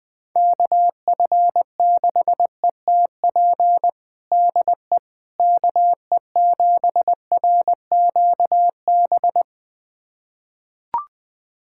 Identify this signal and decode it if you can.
Morse code, heard as a keyed tone, about 20 wpm, 700 Hz. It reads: KF6ETP DE KE7RQB
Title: USA Callsign DE Callsign - 6 Characters - 20wpm